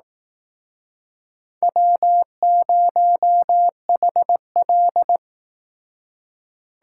Morse code, heard as a keyed tone, about 18 wpm, 700 Hz. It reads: W0HL
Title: Speed Racing - USA 1x2 Call Signs 15wpm